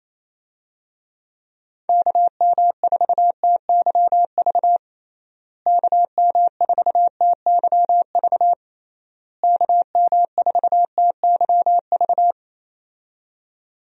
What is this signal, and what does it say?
Morse code, heard as a keyed tone, about 28 wpm, 700 Hz. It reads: KM4TYV KM4TYV KM4TYV
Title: USA Call Signs - 6 Characters - Repeated 3x - 28wpm